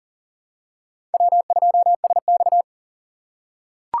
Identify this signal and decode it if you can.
Morse code, heard as a keyed tone, about 39 wpm, 700 Hz. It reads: W2SX
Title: USA Call Signs - 4 Characters - 3x Character Spacing - 40wpm